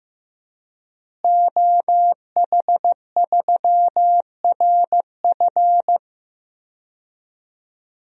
Morse code, heard as a keyed tone, about 15 wpm, 700 Hz. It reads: OH3RF
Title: DXCC-100 Call Signs - 5 Characters - Repeated 2x - 15wpm